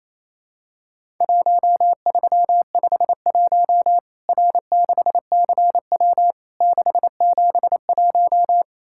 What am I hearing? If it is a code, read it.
Morse code, heard as a keyed tone, about 28 wpm, 700 Hz. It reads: 1351 R6CW 671
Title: Worked All Europe DX Contest - QTC Exchange - 28wpm